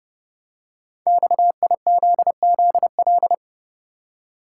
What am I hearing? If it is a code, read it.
Morse code, heard as a keyed tone, about 30 wpm, 700 Hz. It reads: XIZZL
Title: Code Groups - 5 Letters - 30wpm